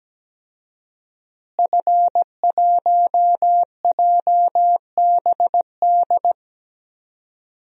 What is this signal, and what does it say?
Morse code, heard as a keyed tone, about 17 wpm, 700 Hz. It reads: F1JBD
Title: DXCC-100 Call Signs - 5 Characters - Repeated 3x - 17wpm